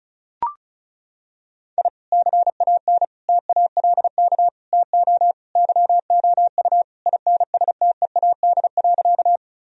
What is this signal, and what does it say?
Morse code, heard as a keyed tone, about 34 wpm, 700 Hz. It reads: I CAN TALK TO YOU INSTEAD.
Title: Sentences from Top 1000 Words - Review Part 4 - 35wpm